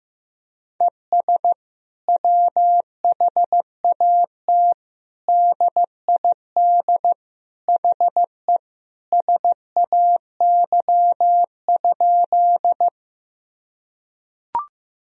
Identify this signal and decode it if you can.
Morse code, heard as a keyed tone, about 15 wpm, 700 Hz. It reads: ES WHAT DID HE SAY?
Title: Sentences from Top 200 Words and Abbreviations 15wpm